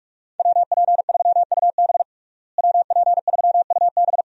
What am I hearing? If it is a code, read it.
Morse code, heard as a keyed tone, about 46 wpm, 700 Hz. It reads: WP3UB WP3UB
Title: USA Call Signs - 5 Characters - Repeated 2x - 45wpm